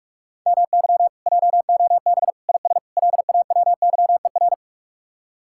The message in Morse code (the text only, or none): MY JOB IS LAWYER